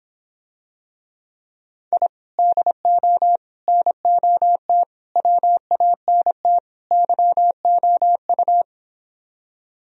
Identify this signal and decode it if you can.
Morse code, heard as a keyed tone, about 26 wpm, 700 Hz. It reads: I DO NOT WANT YOU